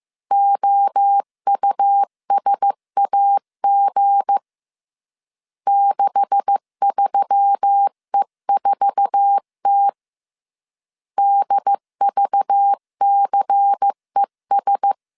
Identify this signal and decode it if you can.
Morse code, heard as a keyed tone, about 15 wpm, 800 Hz. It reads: OUSAG 63E4T DVCES
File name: CW / DET / 1 9 1500C S B 14wpm 3302